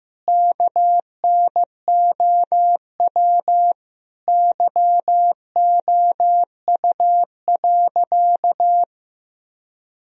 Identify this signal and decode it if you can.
Morse code, heard as a keyed tone, about 15 wpm, 700 Hz. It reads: KNOW YOU.
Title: Sentences from Top 1000 Words - Review Part 1 - 15wpm